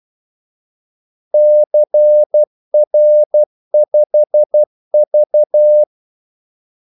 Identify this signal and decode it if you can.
Morse code, heard as a keyed tone, about 12 wpm, 600 Hz. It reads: CR5V
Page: Kaldesignaler 12 wpm